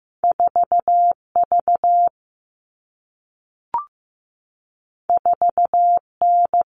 Morse code, heard as a keyed tone, about 15 wpm, 700 Hz. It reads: 4V 4N
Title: All Cut Numbers - 2-Digit 15wpm